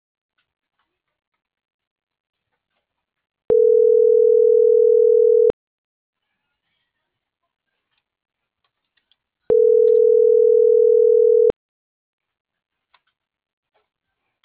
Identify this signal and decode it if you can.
Ringback tone (North America), 440 + 480 Hz.